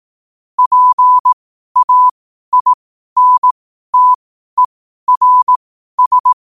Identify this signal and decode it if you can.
Morse code, heard as a keyed tone, about 18 wpm, spaced as for 12 wpm, 1000 Hz. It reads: PAINTERS
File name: qotdfiles / 12wpm